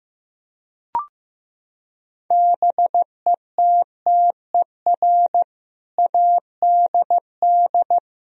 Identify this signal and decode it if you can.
Morse code, heard as a keyed tone, about 15 wpm, 700 Hz. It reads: BETTER ADD
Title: Sentences from Top 1000 Words - Review Part 1 - 15wpm